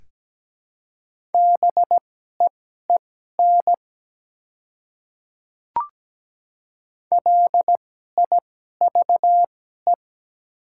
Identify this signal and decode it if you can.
Morse code, heard as a keyed tone, about 17 wpm, spaced as for 12 wpm, 700 Hz. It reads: BEEN LIVE
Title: 2x Character Spacing - Top 300 Words - 17wpm